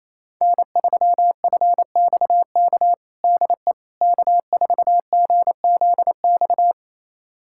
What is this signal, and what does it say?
Morse code, heard as a keyed tone, about 27 wpm, 700 Hz. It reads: N3FXK DE K4GZX
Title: USA Callsign DE Callsign - 5 Characters - 28wpm